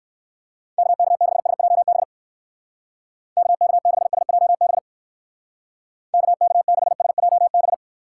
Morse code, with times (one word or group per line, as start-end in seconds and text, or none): 0.8-2.0 KK6SJB
3.4-4.8 KK6SJB
6.1-7.7 KK6SJB